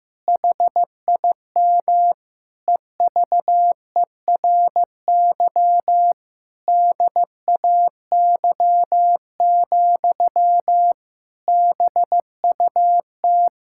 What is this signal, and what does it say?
Morse code, heard as a keyed tone, about 15 wpm, 700 Hz. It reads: HIM EVERY DAY, BUT
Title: Sentences from Top 1600 Words 15wpm